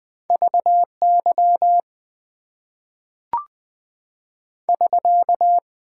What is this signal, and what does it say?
Morse code, heard as a keyed tone, about 20 wpm, 700 Hz. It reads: VY <SK>
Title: Top 90 QSO Elements - 6x Character Spacing - 20wpm